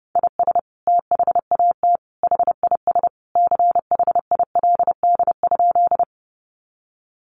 Morse code, heard as a keyed tone, about 30 wpm, 700 Hz. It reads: IS THAT HIS CHILD?
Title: Sentences from Top 800 Words 30wpm